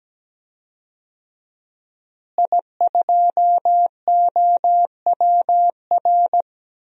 Morse code, heard as a keyed tone, about 17 wpm, 700 Hz. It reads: I2OWR
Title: DXCC-100 Call Signs - 5 Characters - Repeated 2x - 17wpm